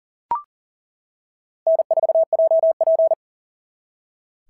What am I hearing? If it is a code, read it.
Morse code, heard as a keyed tone, about 40 wpm, 660 Hz. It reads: N4JP
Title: SOTA QSO Practice 40wpm